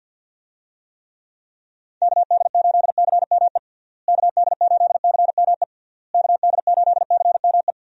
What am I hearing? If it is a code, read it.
Morse code, heard as a keyed tone, about 50 wpm, 700 Hz. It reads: KD8CME KD8CME KD8CME
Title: USA Call Signs - 6 Characters - Repeated 3x - 50wpm